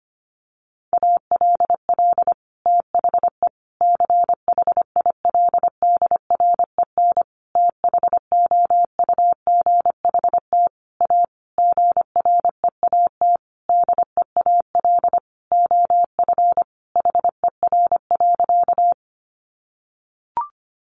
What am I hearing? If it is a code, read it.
Morse code, heard as a keyed tone, about 25 wpm, 700 Hz. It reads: ALL THE CHILDREN THOUGHT A GREAT DEAL OF HER.
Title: Sentences from Top 800 Words 25wpm